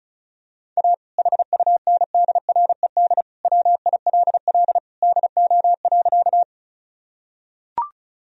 Morse code, heard as a keyed tone, about 36 wpm, 700 Hz. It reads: A HUNDRED WILL DO.
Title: Sentences from Top 1000 Words - Review Part 2 - 35wpm